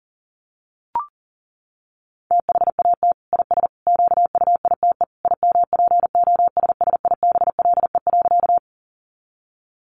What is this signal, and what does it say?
Morse code, heard as a keyed tone, about 37 wpm, 700 Hz. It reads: THAT IS QUITE IMPOSSIBLE.
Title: Sentences from Top 1100 Words 40wpm